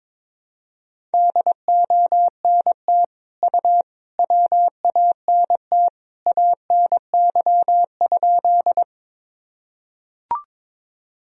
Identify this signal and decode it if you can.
Morse code, heard as a keyed tone, about 22 wpm, 700 Hz. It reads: DONT U WANT ANY?
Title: Sentences from Top 300 Words and Abbreviations 22wpm